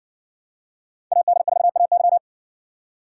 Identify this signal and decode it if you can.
Morse code, heard as a keyed tone, about 60 wpm, 700 Hz. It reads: AD4AY